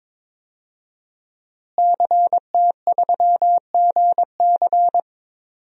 Morse code, heard as a keyed tone, about 22 wpm, 700 Hz. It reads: CT3GC
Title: DXCC-100 Call Signs - 5 Characters - Repeated 3x - 22wpm